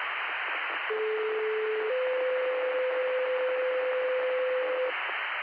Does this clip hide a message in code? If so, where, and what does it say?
Yes: two-tone page from 0.9 to 4.9 s, heard as a one-second tone, then a three-second tone, about 430 then 510 Hz, with noise.